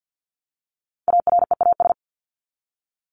Morse code, heard as a keyed tone, about 50 wpm, 700 Hz. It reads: AREAS
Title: ES:S Suffix - Top 100 Words with Base Words 50wpm